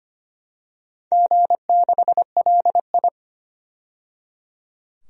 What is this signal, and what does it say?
Morse code, heard as a keyed tone, about 25 wpm, 700 Hz. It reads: G6LI
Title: DXCC-100 Call Signs - 4 Characters - 25wpm